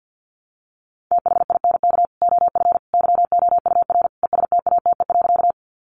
Morse code, heard as a keyed tone, about 50 wpm, 700 Hz. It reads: THINK OF YOUR ESTATE.
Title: Sentences from Top 1200 Words 50wpm